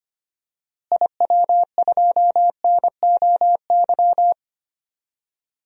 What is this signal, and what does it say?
Morse code, heard as a keyed tone, about 25 wpm, 700 Hz. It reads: IW2NOY
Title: DXCC-100 Call Signs - 6 Characters - Repeated 2x - 25wpm